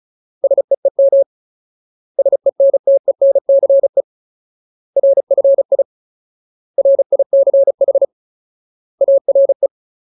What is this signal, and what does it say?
Morse code, heard as a keyed tone, about 34 wpm, 550 Hz. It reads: SEEM SENTENCE RFI RICH ARE
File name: Ultimate Word List / 35WPM Ultimate